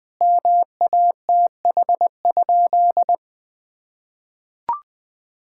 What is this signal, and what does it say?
Morse code, heard as a keyed tone, about 20 wpm, 700 Hz. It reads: MATH?
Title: Sentences from Top 1000 Words - Review Part 6 - 20wpm